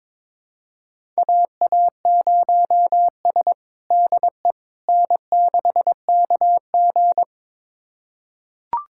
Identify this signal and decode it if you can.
Morse code, heard as a keyed tone, about 21 wpm, 700 Hz. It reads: AA0S DE N6KG
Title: USA Callsign DE Callsign - 4 Characters - 22wpm